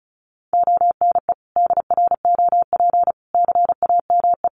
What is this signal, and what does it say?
Morse code, heard as a keyed tone, about 34 wpm, 700 Hz. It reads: ONE DROP CAME
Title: Sentences from Top 1400 Words 35wpm